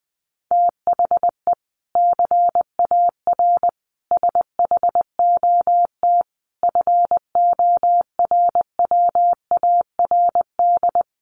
Morse code, heard as a keyed tone, about 20 wpm, 700 Hz. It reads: THE CAR SHOT FORWARD
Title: Sentences from Top 1300 Words 20wpm